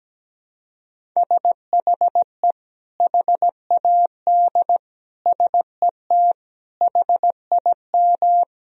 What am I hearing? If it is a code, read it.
Morse code, heard as a keyed tone, about 17 wpm, 700 Hz. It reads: SHE HAD SET HIM
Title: Sentences from Top 1000 Words - Review Part 1 - 17wpm